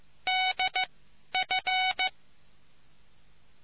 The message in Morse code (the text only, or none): D F